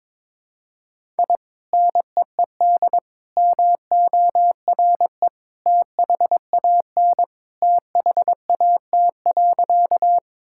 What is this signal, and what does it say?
Morse code, heard as a keyed tone, about 22 wpm, 700 Hz. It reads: I NEED MORE THAN THAT.
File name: practice / 22wpm / Sentences with 2-Word Phrases - 22wpm